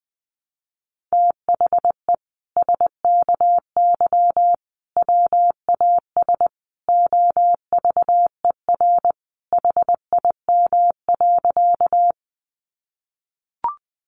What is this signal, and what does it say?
Morse code, heard as a keyed tone, about 20 wpm, 700 Hz. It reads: THE SKY WAS OVER HIM.
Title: Sentences from Top 1200 Words 20wpm